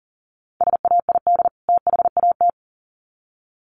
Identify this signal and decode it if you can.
Morse code, heard as a keyed tone, about 40 wpm, 700 Hz. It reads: SAID THAT
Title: Sets of 2 Words 40wpm